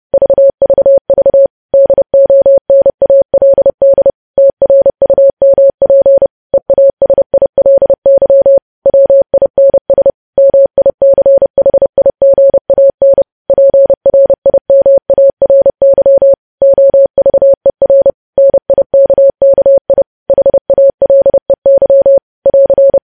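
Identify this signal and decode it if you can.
Morse code, heard as a keyed tone, about 30 wpm, 550 Hz. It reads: VVV DONALD TRUMP EASILY WINS MICHIGAN PRIMARY OVER NIKKI HALEY <AR>
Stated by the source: News headlines sent in Morse code at 30.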